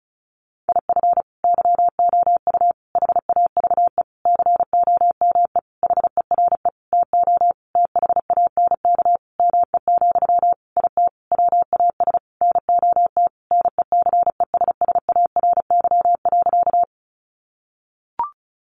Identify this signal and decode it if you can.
Morse code, heard as a keyed tone, about 34 wpm, 700 Hz. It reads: IF YOU HAVE COME HERE TO THANK ME, IT WAS NOT NECESSARY.
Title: Sentences from Top 800 Words 35wpm